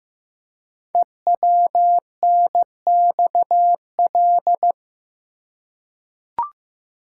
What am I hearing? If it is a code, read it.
Morse code, heard as a keyed tone, about 15 wpm, 700 Hz. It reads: EWNXL
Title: Code Groups - 5 Letters - 15wpm